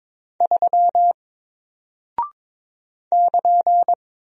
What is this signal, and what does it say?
Morse code, heard as a keyed tone, about 22 wpm, 700 Hz. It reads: 3 <KN>
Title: Single Character (Letter:Number:Punctuation:Prosign) Alt 22wpm